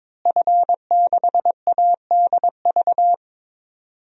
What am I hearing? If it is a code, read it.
Morse code, heard as a keyed tone, about 21 wpm, 700 Hz. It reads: F6ADV
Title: DXCC-100 Call Signs - 5 Characters - Repeated 3x - 22wpm